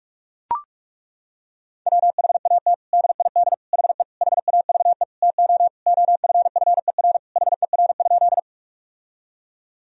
Morse code, heard as a keyed tone, about 42 wpm, 700 Hz. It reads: WHAT DID HE HAVE TO OFFER HER?
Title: Sentences from Top 1000 Words - Review Part 4 - 45wpm